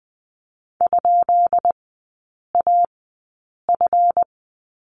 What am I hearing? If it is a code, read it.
Morse code, heard as a keyed tone, about 20 wpm, spaced as for 8 wpm, 700 Hz. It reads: ?AF
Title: 178 Lesson 29e - Slash - Three Characters 20wpm